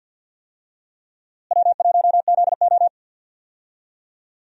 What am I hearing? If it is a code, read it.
Morse code, heard as a keyed tone, about 52 wpm, 700 Hz. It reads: W1ZO
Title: USA Call Signs - 4 Characters - Repeated 2x - 50wpm